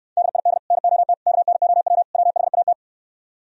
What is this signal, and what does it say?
Morse code, heard as a keyed tone, about 69 wpm, 700 Hz. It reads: DID NOT KNOW WHAT